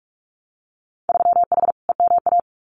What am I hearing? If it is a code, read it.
Morse code, heard as a keyed tone, about 46 wpm, 700 Hz. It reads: 3H EMA
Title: Winter Field Day Exchanges Encore 45wpm